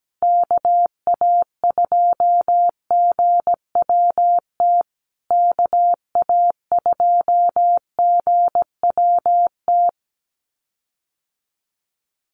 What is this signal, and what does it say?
Morse code, heard as a keyed tone, about 17 wpm, 700 Hz. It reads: KA2GWT KA2GWT
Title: USA Call Signs Repeated 2x 17wpm